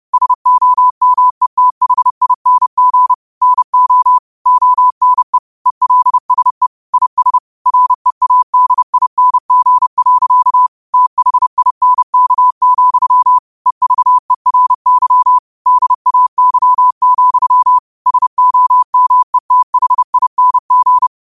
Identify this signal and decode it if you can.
Morse code, heard as a keyed tone, about 30 wpm, 1000 Hz. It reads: SOMETHING NO ONE ELSE IS READING. THINK, EVERY DAY, SOMETHING
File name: qotdfiles / 30wpm